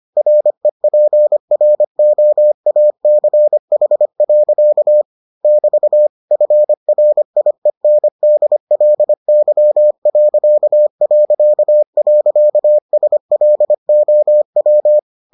Morse code, heard as a keyed tone, about 25 wpm, 600 Hz. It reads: REPROACH. <BT> FRIENDLY...SLOW